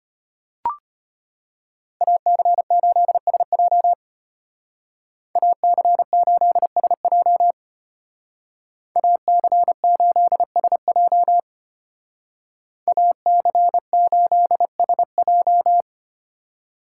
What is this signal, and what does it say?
Morse code, heard as a keyed tone, about 30 wpm, 700 Hz. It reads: AC8SJ AC8SJ AC8SJ AC8SJ
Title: Speed Racing - USA 2x2 Call Signs 25wpm